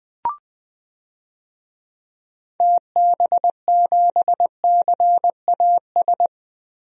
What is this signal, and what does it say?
Morse code, heard as a keyed tone, about 20 wpm, 700 Hz. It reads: TB7CAS
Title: DXCC-100 Call Signs - 6 Characters - 20wpm